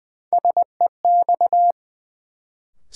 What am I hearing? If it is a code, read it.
Morse code, heard as a keyed tone, about 21 wpm, 700 Hz. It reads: SEX
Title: Sentences from Top 1600 Words - New Word Drill Alt - 20wpm